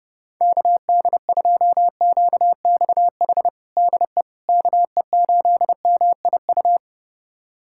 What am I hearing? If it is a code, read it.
Morse code, heard as a keyed tone, about 30 wpm, 700 Hz. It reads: KD2QXH DE KE8MIU
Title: USA Callsign DE Callsign - 6 Characters - 30wpm